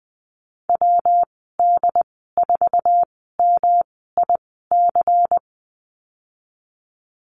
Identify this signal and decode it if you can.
Morse code, heard as a keyed tone, about 20 wpm, spaced as for 15 wpm, 700 Hz. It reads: WD4MIC
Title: 173 Lesson 28g - Number 9 - Call Signs 20wpm